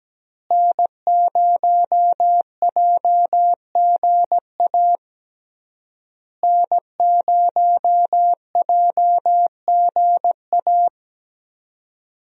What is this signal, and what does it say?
Morse code, heard as a keyed tone, about 17 wpm, 700 Hz. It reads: N0JGA N0JGA
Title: USA Call Signs - 5 Characters - Repeated 3x - 17wpm